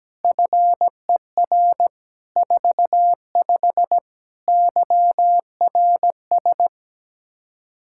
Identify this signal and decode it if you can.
Morse code, heard as a keyed tone, about 17 wpm, 700 Hz. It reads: FER 45 YRS
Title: Practice QSO - Exchange 3 - More Info - 17wpm